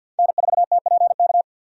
Morse code, heard as a keyed tone, about 52 wpm, 700 Hz. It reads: N4TPK
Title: USA Call Signs - 5 Characters - 50wpm